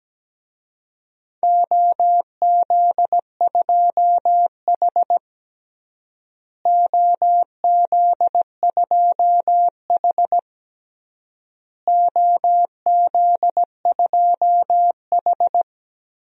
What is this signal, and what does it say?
Morse code, heard as a keyed tone, about 17 wpm, 700 Hz. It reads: OZ2H OZ2H OZ2H
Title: DXCC-100 Call Signs - 4 Characters - Repeated 3x - 17wpm